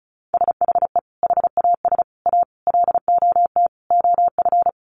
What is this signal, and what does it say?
Morse code, heard as a keyed tone, about 34 wpm, 700 Hz. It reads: SHE HAS A LOT OF
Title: Additional Easy Phrases 35wpm